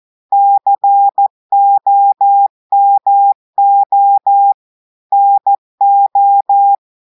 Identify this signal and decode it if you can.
Morse code, heard as a keyed tone, about 14 wpm, 800 Hz. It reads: COMO NO
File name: audios / portugais / portugais texte en clair 1 vitesse 14 mots minutes